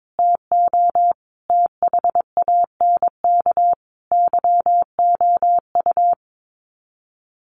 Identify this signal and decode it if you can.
Morse code, heard as a keyed tone, about 22 wpm, 700 Hz. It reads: TO THANK YOU
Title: Sets of 5 Words Encore 22wpm